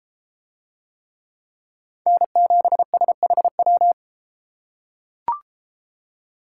Morse code, heard as a keyed tone, about 32 wpm, 700 Hz. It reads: N7SHW